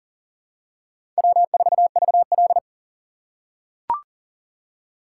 Morse code, heard as a keyed tone, about 40 wpm, 700 Hz. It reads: W4VL
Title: USA Call Signs - 4x Character Spacing - 40wpm